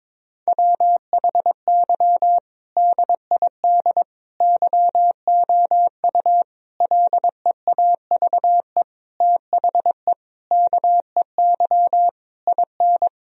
Morse code, heard as a keyed tone, about 22 wpm, 700 Hz. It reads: WHY DID YOU LEAVE THE KEY IN
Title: Sentences from Top 1000 Words - Review Part 4 - 22wpm